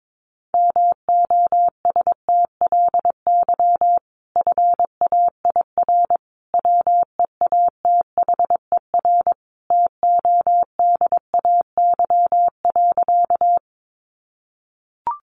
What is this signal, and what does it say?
Morse code, heard as a keyed tone, about 22 wpm, 700 Hz. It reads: MOSTLY FAIR WEATHER TODAY.
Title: Sentences from Top 1300 Words 22wpm